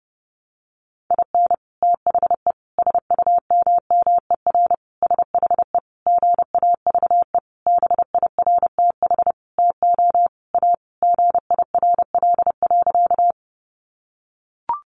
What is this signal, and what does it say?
Morse code, heard as a keyed tone, about 30 wpm, 700 Hz. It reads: IN THE SUMMER SHE GAVE BIRTH TO A GIRL.
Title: Sentences from Top 1100 Words 30wpm